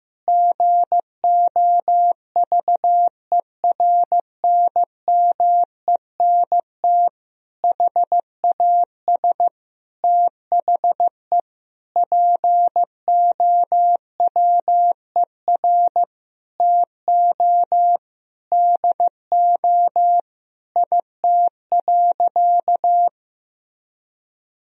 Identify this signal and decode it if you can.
Morse code, heard as a keyed tone, about 15 wpm, 700 Hz. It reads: GOVERNMENT HAS THE POWER TO DO IT.
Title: Sentences from Top 1000 Words - Review Part 3 - 15wpm